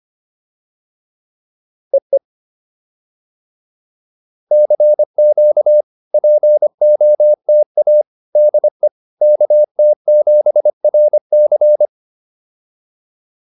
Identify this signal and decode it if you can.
Morse code, heard as a keyed tone, about 25 wpm, 600 Hz. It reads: EE CQ POTA DE KT7RC